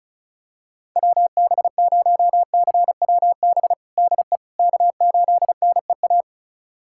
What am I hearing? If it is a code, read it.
Morse code, heard as a keyed tone, about 36 wpm, 700 Hz. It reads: WB0CWB DE K8NEA
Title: USA Callsign DE Callsign 35wpm